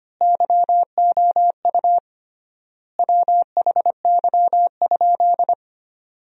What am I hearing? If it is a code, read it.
Morse code, heard as a keyed tone, about 25 wpm, 700 Hz. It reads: YOU WHY?
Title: 3x Word-Spacing - Sentences from Top 200 Words - 25wpm